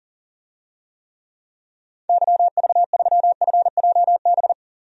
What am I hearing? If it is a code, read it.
Morse code, heard as a keyed tone, about 40 wpm, 700 Hz. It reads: YV3FJB
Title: DXCC-100 Call Signs - 6 Characters - 40wpm